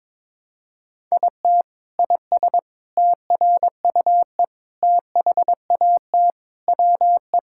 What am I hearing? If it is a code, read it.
Morse code, heard as a keyed tone, about 21 wpm, 700 Hz. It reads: IT IS TRUE THAT WE
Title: Sentences from Top 1600 Words 22wpm